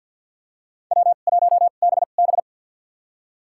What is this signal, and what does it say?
Morse code, heard as a keyed tone, about 53 wpm, spaced as for 37 wpm, 700 Hz. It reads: W1BB
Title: USA Call Signs - 4 Characters - 2x Character Spacing - 50wpm